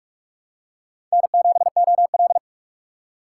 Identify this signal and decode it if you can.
Morse code, heard as a keyed tone, about 42 wpm, 700 Hz. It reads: N7OL